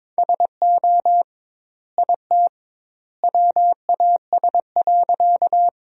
Morse code, heard as a keyed tone, about 21 wpm, 700 Hz. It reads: SO IT WAS.
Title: 2x Word-Spacing - Sentences from Top 100 Words - 22wpm